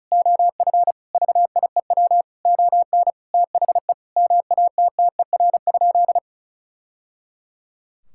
Morse code, heard as a keyed tone, about 34 wpm, 700 Hz. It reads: OF VIEW ON THE MATTER?
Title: Sentences from Top 1000 Words - Review Part 6 - 35wpm